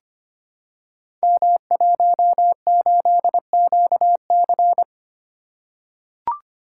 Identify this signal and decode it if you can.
Morse code, heard as a keyed tone, about 26 wpm, 700 Hz. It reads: M18QC
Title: Code Groups - 5 Letter:Numbers - 25wpm